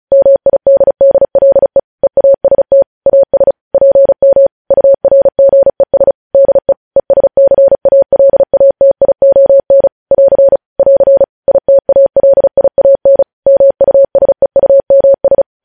Morse code, heard as a keyed tone, about 36 wpm, 550 Hz. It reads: MIDDLE EAST AS PM URGES DE ESCALATION <AR> <AR> ITALIAN MUSEUMS
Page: news headlines in morse code at 35 wpm